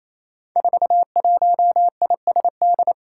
Morse code, heard as a keyed tone, about 28 wpm, 700 Hz. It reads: 41ISD
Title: Five Challenging Letter-Number Combinations 28wpm